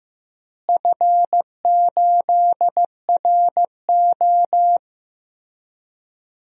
Morse code, heard as a keyed tone, about 15 wpm, 700 Hz. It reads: F8RO
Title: DXCC-100 Call Signs - 4 Characters - Repeated 3x - 15wpm